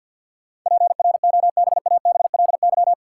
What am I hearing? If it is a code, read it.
Morse code, heard as a keyed tone, about 53 wpm, 700 Hz. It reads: PROBABLY
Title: Sentences from Top 700 Words - New Word Drill - 50wpm